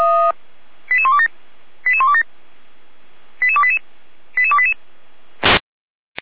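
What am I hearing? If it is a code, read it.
Five-tone selective call (ZVEI), heard as quick runs of five tones, tones of about 70 ms. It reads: 88127 88127 88288 88288
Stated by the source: ZVEI five-tone selective call 88289, received on an SDR.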